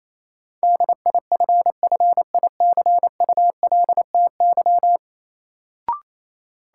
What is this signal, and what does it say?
Morse code, heard as a keyed tone, about 28 wpm, 700 Hz. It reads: DIFFICULTY
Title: Sentences from Top 1500 Words - New Word Drill - 28wpm